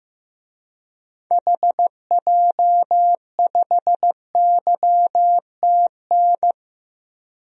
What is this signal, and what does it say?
Morse code, heard as a keyed tone, about 15 wpm, 700 Hz. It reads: HJ5YTN
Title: DXCC-100 Call Signs - 6 Characters - 15wpm